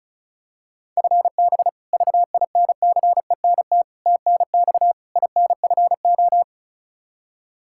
Morse code, heard as a keyed tone, about 36 wpm, 700 Hz. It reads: FB VINCENT TNX INFO
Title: Practice QSO - Exchange 3 - More Info - 35wpm